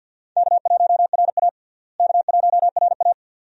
Morse code, heard as a keyed tone, about 52 wpm, 700 Hz. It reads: K1RA K1RA
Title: USA Call Signs - 4 Characters - Repeated 2x - 50wpm